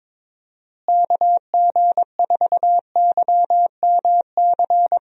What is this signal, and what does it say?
Morse code, heard as a keyed tone, about 21 wpm, 700 Hz. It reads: KG4YMC
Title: USA Call Signs - 6 Characters - Repeated 2x - 22wpm